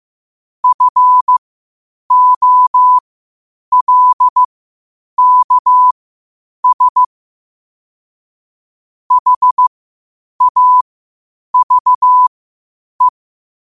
Morse code, heard as a keyed tone, about 15 wpm, spaced as for 8 wpm, 1000 Hz. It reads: FOLKS HAVE
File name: qotdfiles / 8wpm